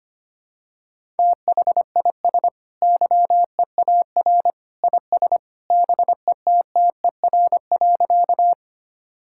Morse code, heard as a keyed tone, about 25 wpm, 700 Hz. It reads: THIS YEAR IS BETTER.
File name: practice / 25wpm / Sentences with 2-Word Phrases - 25wpm